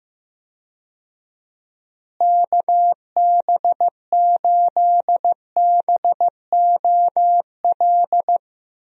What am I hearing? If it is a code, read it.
Morse code, heard as a keyed tone, about 15 wpm, 700 Hz. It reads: KB8BOL
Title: USA Call Signs - 6 Characters - Repeated 2x - 15wpm